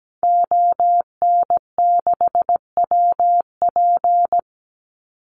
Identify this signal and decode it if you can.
Morse code, heard as a keyed tone, about 17 wpm, 700 Hz. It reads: ON6WP
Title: Call Signs - 17wpm